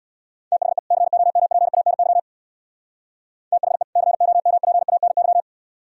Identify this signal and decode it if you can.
Morse code, heard as a keyed tone, about 59 wpm, 700 Hz. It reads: THE COMPANY THE COMPANY
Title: Speed-Racing - Sets of 2 Words 50wpm